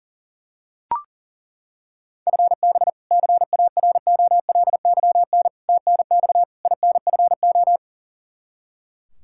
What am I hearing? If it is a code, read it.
Morse code, heard as a keyed tone, about 40 wpm, 690 Hz. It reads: FB CAROLYN TNX INFO
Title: Practice QSO - Exchange 3 - More Info - 40wpm